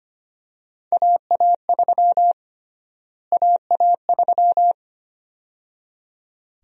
Morse code, heard as a keyed tone, about 25 wpm, 700 Hz. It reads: AA3 AA3
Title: USA Call Sign Prefixes - Repeated 2x - 25wpm